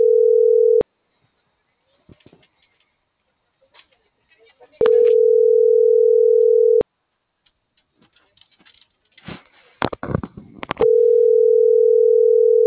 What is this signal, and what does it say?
Ringback tone (North America), 440 + 480 Hz.
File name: CENTRAL MORENA / out dulce.morena.2024